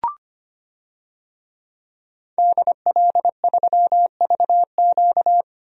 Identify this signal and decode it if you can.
Morse code, heard as a keyed tone, about 25 wpm, 700 Hz. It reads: DL3VQ
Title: DXCC-100 Call Signs - 5 Characters - Repeated 2x - 25wpm